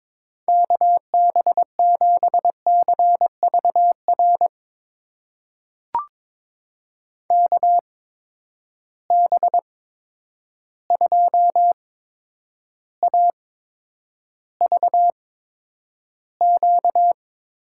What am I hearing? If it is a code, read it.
Morse code, heard as a keyed tone, about 22 wpm, spaced as for 9 wpm, 700 Hz. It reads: KB7CVR KB2AVQ
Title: USA Call Signs - 6 Characters - 8x Character Spacing - 22wpm